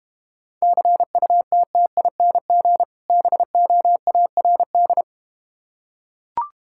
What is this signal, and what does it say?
Morse code, heard as a keyed tone, about 32 wpm, 700 Hz. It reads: CUTTING BOARD